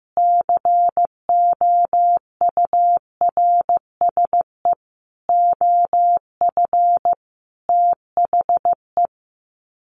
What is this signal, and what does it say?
Morse code, heard as a keyed tone, about 15 wpm, 700 Hz. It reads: COURSE OF THE
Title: Sets of 5 Words 15wpm